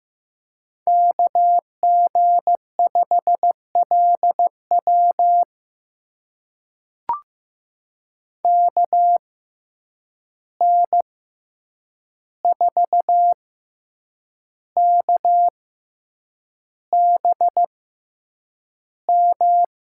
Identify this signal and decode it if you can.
Morse code, heard as a keyed tone, about 15 wpm, spaced as for 7 wpm, 700 Hz. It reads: KG5LW KN4KBM
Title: USA Call Signs - 6x Character Spacing - 15wpm